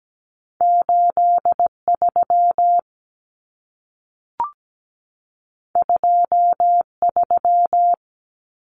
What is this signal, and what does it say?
Morse code, heard as a keyed tone, about 17 wpm, 700 Hz. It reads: 83 23
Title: Numbers - 2-Digit 17wpm